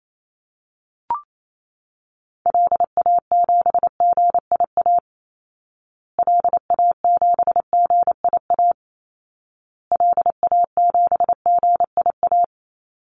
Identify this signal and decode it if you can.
Morse code, heard as a keyed tone, about 28 wpm, 700 Hz. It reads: LA7GIA LA7GIA LA7GIA
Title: Call Signs Repeated 2x - 28wpm